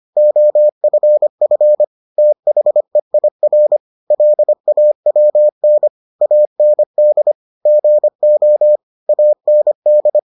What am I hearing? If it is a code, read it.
Morse code, heard as a keyed tone, about 25 wpm, 600 Hz. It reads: OFF THEIR LAWN AND GO AND